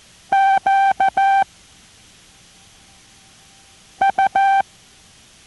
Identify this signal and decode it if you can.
Morse code, heard as a keyed tone, about 14 wpm, 780 Hz. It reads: Q U